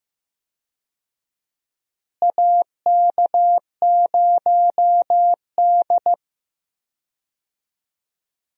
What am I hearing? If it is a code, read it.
Morse code, heard as a keyed tone, about 15 wpm, 700 Hz. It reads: AK0D
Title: USA Call Signs - 4 Characters - Repeated 3x - 15wpm